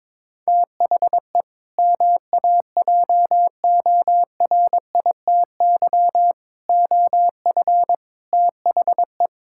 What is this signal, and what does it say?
Morse code, heard as a keyed tone, about 22 wpm, 700 Hz. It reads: THE MAJORITY OF THE
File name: practice / 22wpm / Sentences with 4-Word Phrases - 22wpm